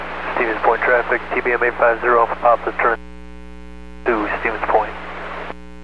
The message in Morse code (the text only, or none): none